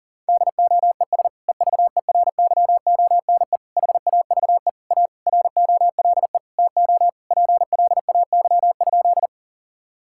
Morse code, heard as a keyed tone, about 37 wpm, 700 Hz. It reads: DOES EVERYONE HAVE A ROLE TO PLAY?
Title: Sentences from Top 1000 Words - Review Part 6 - 40wpm